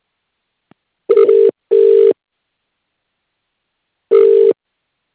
Ringback tone (UK), 400 + 450 Hz.